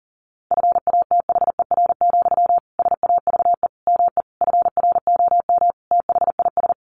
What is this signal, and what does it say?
Morse code, heard as a keyed tone, about 40 wpm, 700 Hz. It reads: FATHER, SAVE ME FROM THIS
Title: Sentences from Top 1100 Words 40wpm